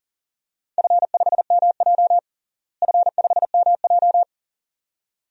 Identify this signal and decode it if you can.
Morse code, heard as a keyed tone, about 40 wpm, 700 Hz. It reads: F5MJ F5MJ
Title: DXCC-100 Call Signs - 4 Characters - Repeated 2x - 40wpm